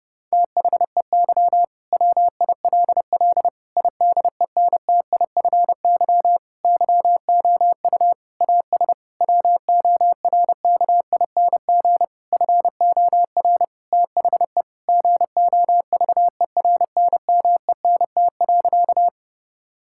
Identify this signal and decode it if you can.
Morse code, heard as a keyed tone, about 30 wpm, 700 Hz. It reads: THEY WILL IDENTIFY YOU AS WORKING FOR THE GOVERNMENT.
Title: Sentences from Top 1000 Words - Review Part 6 - 30wpm